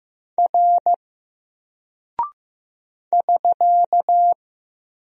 Morse code, heard as a keyed tone, about 15 wpm, 700 Hz. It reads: R <SK>
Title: Single Character (Letter:Number:Punctuation:Prosign) 15wpm